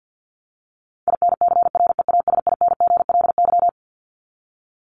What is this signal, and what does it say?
Morse code, heard as a keyed tone, about 50 wpm, 700 Hz. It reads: INCREASINGLY
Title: LY Suffix - Top 100 Words with Base Words 50wpm